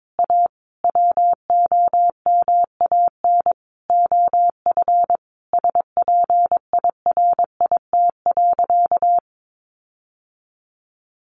Morse code, heard as a keyed tone, about 22 wpm, 700 Hz. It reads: A WOMAN OF SPIRIT.
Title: Sentences from Top 1400 Words 22wpm